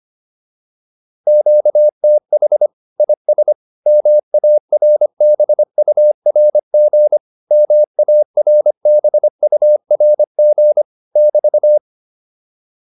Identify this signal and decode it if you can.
Morse code, heard as a keyed tone, about 25 wpm, 600 Hz. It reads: QTH IS MARBURG MARBURG <BT>
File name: QSO405a 25